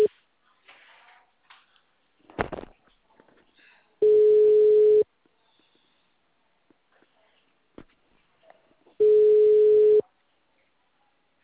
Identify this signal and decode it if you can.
Ringback tone (Europe, 425 Hz).
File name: out- -2009- - - .497